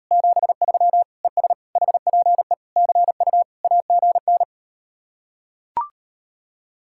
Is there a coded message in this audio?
Yes: Morse code from 0.1 to 4.4 s, heard as a keyed tone, about 38 wpm, 700 Hz. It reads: 73 ES HPE CU AGN